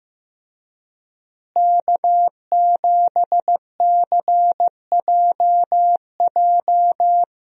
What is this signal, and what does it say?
Morse code, heard as a keyed tone, about 15 wpm, 700 Hz. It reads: K7CJJ
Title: USA Call Signs - 5 Characters - Repeated 3x - 15wpm